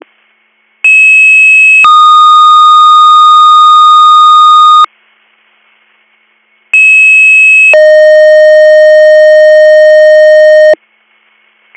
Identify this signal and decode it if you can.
Two-tone page, heard as a one-second tone, then a three-second tone, about 2600 then 1230 Hz, about 2600 then 620 Hz. It is heard from 0.8 to 10.7 s.